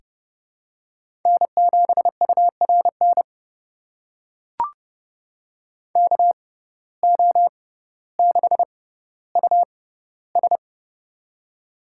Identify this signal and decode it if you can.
Morse code, heard as a keyed tone, about 30 wpm, spaced as for 15 wpm, 700 Hz. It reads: N7URN KO6US
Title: USA Call Signs - 6x Character Spacing - 30wpm